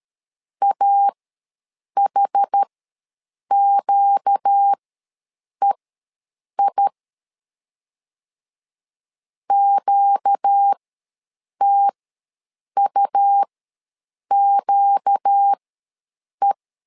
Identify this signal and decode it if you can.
Morse code, heard as a keyed tone, about 13 wpm, spaced as for 7 wpm, 800 Hz. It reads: AHQEI QTUQE
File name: LEZIONI / 4 Q ws 12wpm 1912